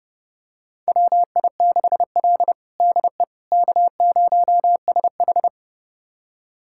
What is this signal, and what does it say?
Morse code, heard as a keyed tone, about 30 wpm, 700 Hz. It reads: WI6L DE K0SH
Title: USA Callsign DE Callsign - 4 Characters - 30wpm